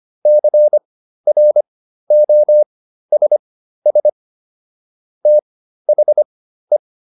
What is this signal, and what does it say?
Morse code, heard as a keyed tone, about 25 wpm, spaced as for 13 wpm, 600 Hz. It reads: CROSS THE
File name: PR402 13